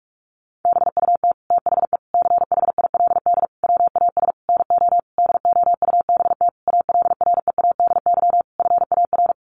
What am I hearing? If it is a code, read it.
Morse code, heard as a keyed tone, about 44 wpm, 700 Hz. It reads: BUT THE CHILD WAS NO DOUBT ALREADY FAR
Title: Sentences from Top 1500 Words 45wpm